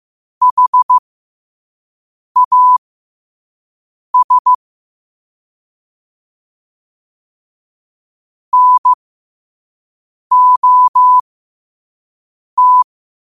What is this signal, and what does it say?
Morse code, heard as a keyed tone, about 15 wpm, spaced as for 5 wpm, 1000 Hz. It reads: HAS NOT